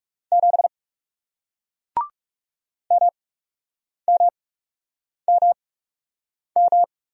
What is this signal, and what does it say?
Morse code, heard as a keyed tone, about 40 wpm, spaced as for 8 wpm, 700 Hz. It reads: 7 MMMM